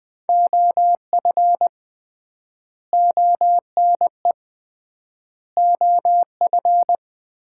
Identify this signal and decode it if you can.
Morse code, heard as a keyed tone, about 20 wpm, 700 Hz. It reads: OF ONE OF
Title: 3x Word-Spacing - Sets of 4 Words - 20wpm